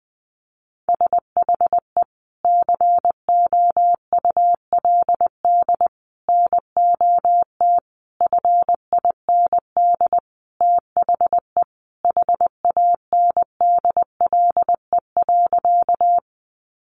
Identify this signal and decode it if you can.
Morse code, heard as a keyed tone, about 20 wpm, 700 Hz. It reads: SHE COULD NOT FIND THE HANDLE.
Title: Sentences from Top 1100 Words 20wpm